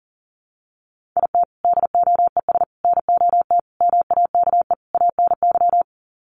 Morse code, heard as a keyed tone, about 40 wpm, 700 Hz. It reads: IT DOES NOT MAKE ANY
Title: Sets of 5 Words Encore2 40wpm